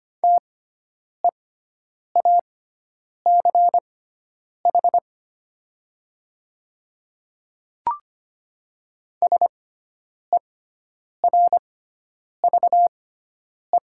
Morse code, heard as a keyed tone, about 25 wpm, spaced as for 9 wpm, 700 Hz. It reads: TEACH SERVE
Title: 6x Character Spacing - Top 500 Words - 25wpm